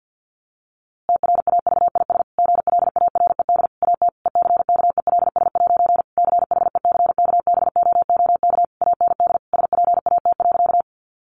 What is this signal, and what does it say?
Morse code, heard as a keyed tone, about 50 wpm, 700 Hz. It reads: TRAVIS GLARED AT ECKELS' CHECKBOOK AND SPAT.
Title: A Sound Of Thunder 50wpm